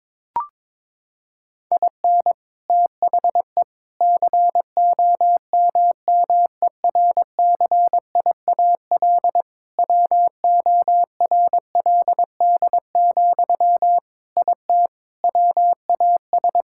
Morse code, heard as a keyed tone, about 22 wpm, 700 Hz. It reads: IN THE COMMERCIAL WORLD, IT WAS
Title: Sentences from Top 1000 Words - Review Part 5 - 22wpm